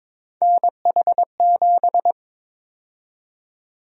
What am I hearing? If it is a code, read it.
Morse code, heard as a keyed tone, about 21 wpm, 700 Hz. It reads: NH7
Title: USA Call Sign Prefixes - Repeated 2x - 22wpm